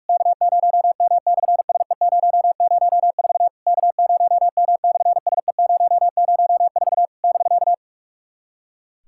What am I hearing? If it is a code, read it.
Morse code, heard as a keyed tone, about 46 wpm, 700 Hz. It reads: K0M/SE004 K0M/SE004 <BK>
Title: SOTA QSO Practice 45wpm